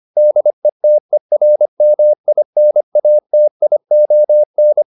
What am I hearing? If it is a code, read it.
Morse code, heard as a keyed tone, about 25 wpm, 600 Hz. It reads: DETERMINATION